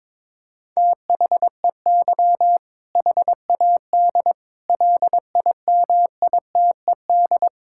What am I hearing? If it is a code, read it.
Morse code, heard as a keyed tone, about 22 wpm, 700 Hz. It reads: THEY HAD LIMITED
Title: Sentences from Top 1000 Words - Review Part 6 - 22wpm